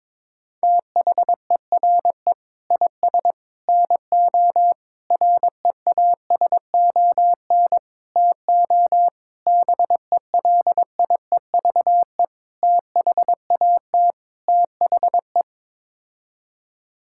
Morse code, heard as a keyed tone, about 22 wpm, 700 Hz. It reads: THERE IS NO REASON TO BELIEVE THAT THE
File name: practice / 22wpm / Sets of 8 Words - 22wpm